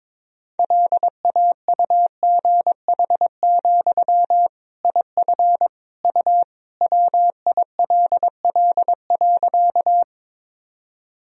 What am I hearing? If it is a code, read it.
Morse code, heard as a keyed tone, about 22 wpm, 700 Hz. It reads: LAUGH, IF U WILL.
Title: Sentences from Top 500 Words and Abbreviations 22wpm